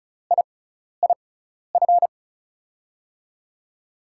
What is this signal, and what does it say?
Morse code, heard as a keyed tone, about 36 wpm, spaced as for 12 wpm, 700 Hz. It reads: IIF
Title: USA Call Signs - 6 Characters - 6x Character Spacing - 35wpm